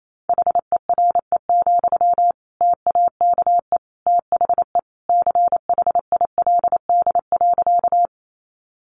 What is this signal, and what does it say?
Morse code, heard as a keyed tone, about 28 wpm, 700 Hz. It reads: HERE, TAKE THE CHILD.
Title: Sentences from Top 800 Words 28wpm